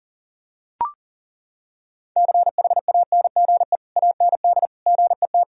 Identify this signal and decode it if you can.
Morse code, heard as a keyed tone, about 40 wpm, 700 Hz. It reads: CHANGE AND GET
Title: Sentences from Top 1000 Words - Review Part 1 - 40wpm